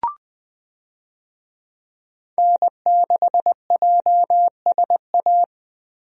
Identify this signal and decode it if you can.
Morse code, heard as a keyed tone, about 20 wpm, 700 Hz. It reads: N6JSA
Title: USA Call Signs - 5 Characters - Repeated 3x - 20wpm